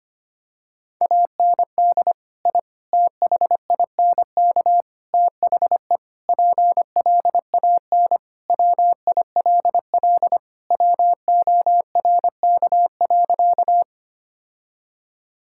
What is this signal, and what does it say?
Morse code, heard as a keyed tone, about 25 wpm, 700 Hz. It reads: AND I THINK THE PLAN WILL WORK.
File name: practice / 25wpm / Sentences with 4-Word Phrases - 25wpm